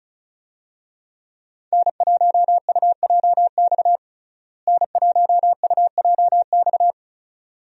Morse code, heard as a keyed tone, about 34 wpm, 700 Hz. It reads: N1UJX N1UJX
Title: USA Call Signs - 5 Characters - Repeated 2x - 35wpm